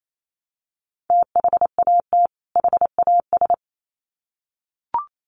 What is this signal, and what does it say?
Morse code, heard as a keyed tone, about 28 wpm, 700 Hz. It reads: THAT HAS
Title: Sets of 2 Words 28wpm